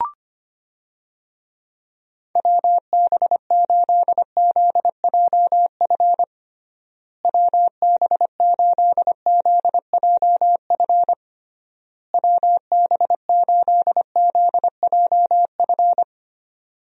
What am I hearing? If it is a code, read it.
Morse code, heard as a keyed tone, about 25 wpm, 700 Hz. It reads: WB8ZJF WB8ZJF WB8ZJF
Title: USA Call Signs - 6 Characters - Repeated 3x - 25wpm